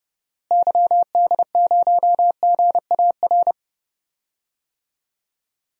Morse code, heard as a keyed tone, about 31 wpm, 700 Hz. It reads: YD0GAR
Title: DXCC-100 Call Signs - 6 Characters - Repeated 2x - 30wpm